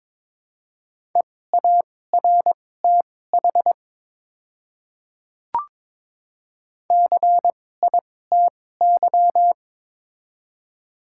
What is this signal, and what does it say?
Morse code, heard as a keyed tone, about 22 wpm, spaced as for 16 wpm, 700 Hz. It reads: EARTH CITY
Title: 2x Character Spacing - Top 300 Words - 22wpm